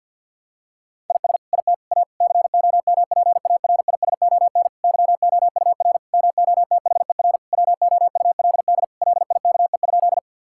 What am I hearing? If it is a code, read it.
Morse code, heard as a keyed tone, about 50 wpm, 700 Hz. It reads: IS IT A COMPARISON YOUR MOTHER WOULD LIKE?
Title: Sentences from Top 1600 Words 50wpm